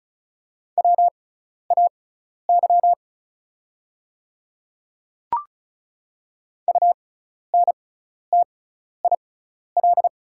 Morse code, heard as a keyed tone, about 36 wpm, spaced as for 12 wpm, 690 Hz. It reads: WAY UNTIL
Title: 6x Character Spacing - Top 300 Words - 35wpm